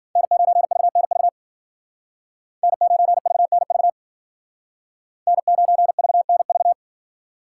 Morse code, heard as a keyed tone, about 51 wpm, 700 Hz. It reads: N9VNV N9VNV N9VNV